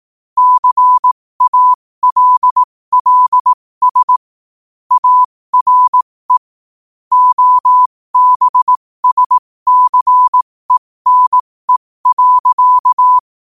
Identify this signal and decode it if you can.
Morse code, heard as a keyed tone, about 18 wpm, spaced as for 15 wpm, 1000 Hz. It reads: CALLS ARE OBSCENE.
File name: qotdfiles / 15wpm